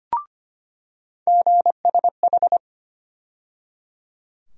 Morse code, heard as a keyed tone, about 25 wpm, 700 Hz. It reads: GSH
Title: USA Call Sign Postfixes - 3 Letters 25wpm